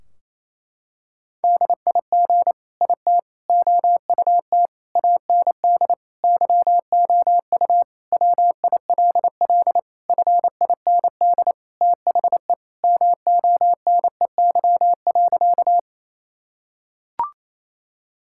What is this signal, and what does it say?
Morse code, heard as a keyed tone, about 27 wpm, 700 Hz. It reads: DIG IT OUT AND YOU WILL FIND THE MONEY.
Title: Sentences from Top 1600 Words 28wpm